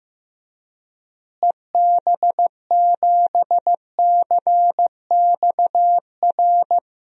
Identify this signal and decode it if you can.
Morse code, heard as a keyed tone, about 15 wpm, 700 Hz. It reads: EB7CXR
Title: DXCC-100 Call Signs - 6 Characters - Repeated 3x - 15wpm